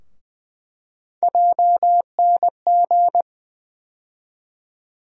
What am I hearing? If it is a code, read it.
Morse code, heard as a keyed tone, about 20 wpm, 700 Hz. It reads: JNG
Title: USA Call Sign Postfixes - 3 Letters 20wpm